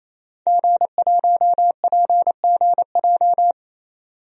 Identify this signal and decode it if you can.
Morse code, heard as a keyed tone, about 27 wpm, 700 Hz. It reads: G1PGJ
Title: DXCC-100 Call Signs - 5 Characters - 28wpm